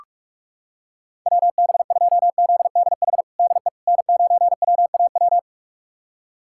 Morse code, heard as a keyed tone, about 46 wpm, 700 Hz. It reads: WB2ZDH DE N9WAW
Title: USA Callsign DE Callsign 45wpm